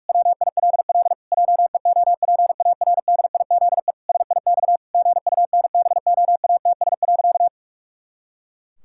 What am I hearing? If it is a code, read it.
Morse code, heard as a keyed tone, about 46 wpm, 700 Hz. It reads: WILL JEOPARDIZE SIX GUNBOATS.
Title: Pangram Challenge v2 - 45wpm